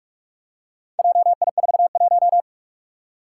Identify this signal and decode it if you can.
Morse code, heard as a keyed tone, about 42 wpm, 700 Hz. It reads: JI41
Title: Four Challenging Letter-Number Combinations 45wpm